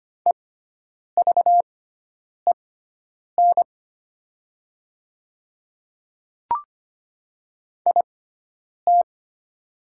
Morse code, heard as a keyed tone, about 25 wpm, spaced as for 9 wpm, 690 Hz. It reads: EVEN IT
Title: 6x Character Spacing - Top 200 Words - 25wpm